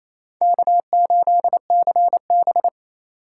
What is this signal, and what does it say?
Morse code, heard as a keyed tone, about 36 wpm, 700 Hz. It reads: K8CB